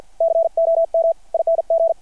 Morse code, heard as a keyed tone, about 50 wpm, spaced as for 41 wpm, 660 Hz. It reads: COMING
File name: archived / CRSnet